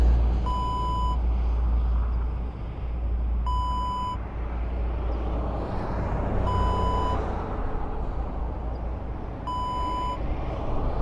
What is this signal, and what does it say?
There is no code.